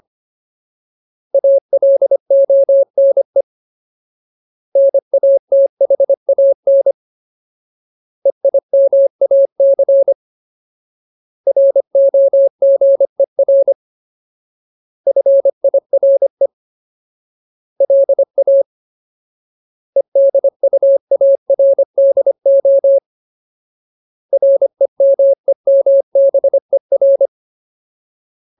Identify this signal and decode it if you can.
Morse code, heard as a keyed tone, about 25 wpm, 550 Hz. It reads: ALONE NATHAN EIMAC ROGER FIRE LA EDUARDO REMEMBER
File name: Ultimate Word List / 25WPM Ultimate